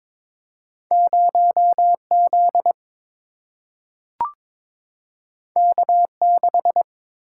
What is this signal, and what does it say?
Morse code, heard as a keyed tone, about 22 wpm, 700 Hz. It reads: 0 Z K 6
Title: Single-Letter Single-Number 22wpm